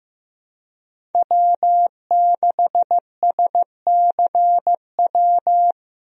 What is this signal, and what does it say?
Morse code, heard as a keyed tone, about 15 wpm, 700 Hz. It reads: W6SCW
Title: USA Call Signs - 5 Characters - Repeated 2x - 15wpm